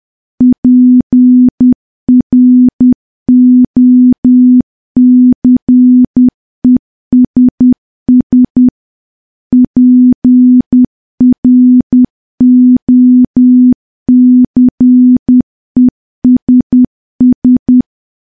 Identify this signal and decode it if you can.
Morse code, heard as a keyed tone, about 10 wpm, 250 Hz. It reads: PROCESS PROCESS
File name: morse / morse-training-10WPM